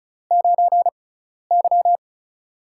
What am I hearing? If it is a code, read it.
Morse code, heard as a keyed tone, about 34 wpm, spaced as for 12 wpm, 700 Hz. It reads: 9Y
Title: USA Call Signs - 6x Character Spacing - 35wpm